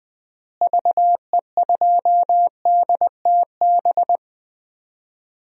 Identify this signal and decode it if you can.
Morse code, heard as a keyed tone, about 21 wpm, 700 Hz. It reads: VE2DTB
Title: DXCC-100 Call Signs - 6 Characters - Repeated 3x - 20wpm